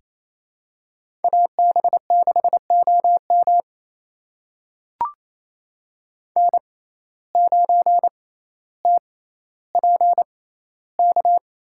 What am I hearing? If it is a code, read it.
Morse code, heard as a keyed tone, about 27 wpm, spaced as for 14 wpm, 700 Hz. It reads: AB6OM N9TPK
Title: USA Call Signs - 5 Characters - 6x Character Spacing - 28wpm